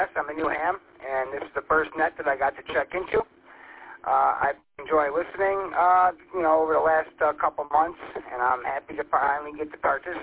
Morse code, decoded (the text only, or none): none